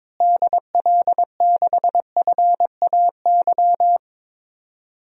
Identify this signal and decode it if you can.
Morse code, heard as a keyed tone, about 22 wpm, 700 Hz. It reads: DL6FAY
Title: DXCC-100 Call Signs - 6 Characters - Repeated 3x - 22wpm